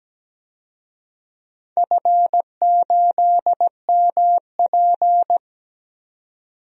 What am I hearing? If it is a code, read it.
Morse code, heard as a keyed tone, about 17 wpm, 700 Hz. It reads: F8MP
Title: DXCC-100 Call Signs - 4 Characters - 17wpm